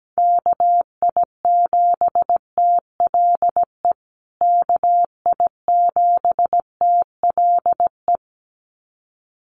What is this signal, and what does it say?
Morse code, heard as a keyed tone, about 17 wpm, 700 Hz. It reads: KI7TLE KI7TLE
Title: USA Call Signs Repeated 2x 17wpm